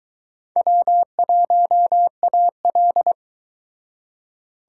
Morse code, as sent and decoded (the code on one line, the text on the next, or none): .-- .---- .- .-..
W1AL